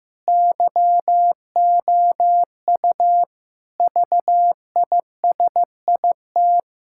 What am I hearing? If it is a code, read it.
Morse code, heard as a keyed tone, about 15 wpm, 700 Hz. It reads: YOU VISIT
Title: Sentences from Top 1000 Words - Review Part 6 - 15wpm